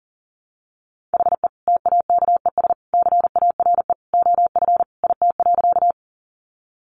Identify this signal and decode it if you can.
Morse code, heard as a keyed tone, about 40 wpm, 700 Hz. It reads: HE TAKES CARE OF IT.
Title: Additional Easy Phrases 40wpm